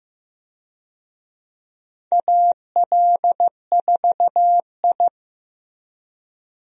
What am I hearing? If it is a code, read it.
Morse code, heard as a keyed tone, about 15 wpm, 700 Hz. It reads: AL4I
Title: USA Call Signs - 4 Characters - 15wpm